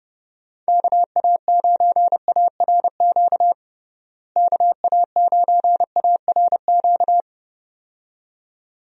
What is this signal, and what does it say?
Morse code, heard as a keyed tone, about 32 wpm, 700 Hz. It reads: KA9ARQ KA9ARQ
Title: USA Call Signs - 6 Characters - Repeated 2x - 30wpm